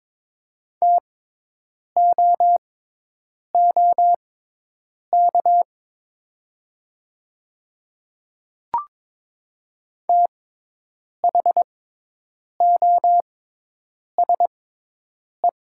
Morse code, heard as a keyed tone, about 22 wpm, spaced as for 8 wpm, 700 Hz. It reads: TOOK THOSE
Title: 6x Character Spacing - Top 300 Words - 22wpm